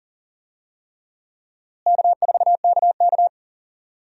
Morse code, heard as a keyed tone, about 40 wpm, 700 Hz. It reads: K4KK
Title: USA Call Signs - 4 Characters - Repeated 3x - 40wpm